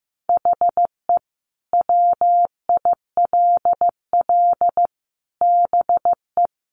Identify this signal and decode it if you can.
Morse code, heard as a keyed tone, about 15 wpm, 700 Hz. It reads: HE WILL BE
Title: Sentences from Top 1100 Words 15wpm